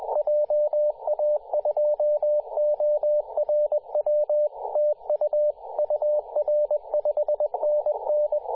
Morse code, heard as a keyed tone, about 20 wpm, 600 Hz, with radio noise. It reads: JA2ORW TU UR 5NN
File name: posts / 2017-sep-21-0819zt88xa-10-116mhz-cw